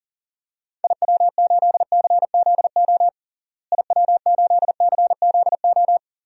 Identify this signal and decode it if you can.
Morse code, heard as a keyed tone, about 40 wpm, 700 Hz. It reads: IW8CZO IW8CZO
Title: DXCC-100 Call Signs - 6 Characters - Repeated 2x - 40wpm